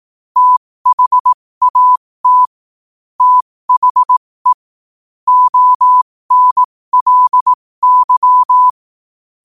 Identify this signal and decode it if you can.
Morse code, heard as a keyed tone, about 18 wpm, spaced as for 15 wpm, 1000 Hz. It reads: THAT THE ONLY
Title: 15 WPM morse code quotes for Fri, 01 Mar 2024